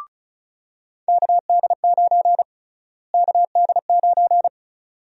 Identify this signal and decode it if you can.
Morse code, heard as a keyed tone, about 34 wpm, 700 Hz. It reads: KD9 KD9
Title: USA Call Sign Prefixes - Repeated 2x - 35wpm